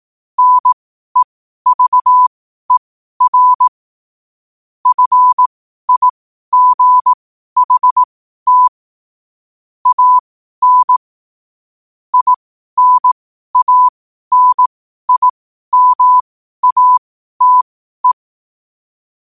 Morse code, heard as a keyed tone, about 18 wpm, spaced as for 12 wpm, 1000 Hz. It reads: NEVER FIGHT AN INANIMATE
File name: qotdfiles / 12wpm